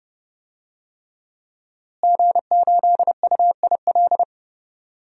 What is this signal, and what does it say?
Morse code, heard as a keyed tone, about 32 wpm, 700 Hz. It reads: G8UIL
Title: DXCC-100 Call Signs - 5 Characters - Repeated 3x - 30wpm